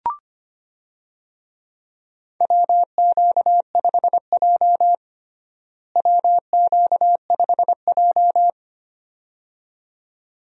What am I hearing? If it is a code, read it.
Morse code, heard as a keyed tone, about 25 wpm, 700 Hz. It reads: WQ5J WQ5J
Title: USA Call Signs - 4 Characters - Repeated 2x - 25wpm